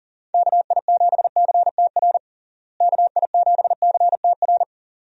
Morse code, heard as a keyed tone, about 40 wpm, 700 Hz. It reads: KI7CTR KI7CTR
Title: USA Call Signs - 6 Characters - Repeated 2x - 40wpm